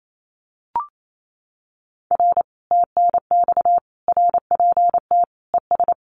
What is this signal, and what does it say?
Morse code, heard as a keyed tone, about 28 wpm, 700 Hz. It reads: R TNX RPT ES
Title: Common QSO Phrases Encore 28wpm